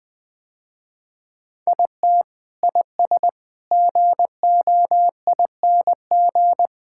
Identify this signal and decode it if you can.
Morse code, heard as a keyed tone, about 20 wpm, 700 Hz. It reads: IT IS GOING
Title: Sentences with 4-Word Phrases - Spoken Beginning - 20wpm